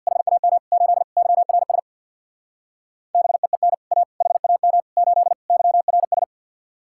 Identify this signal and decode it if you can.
Morse code, heard as a keyed tone, about 63 wpm, 700 Hz. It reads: HAM 8 YRS BEEN A HAM 8 YRS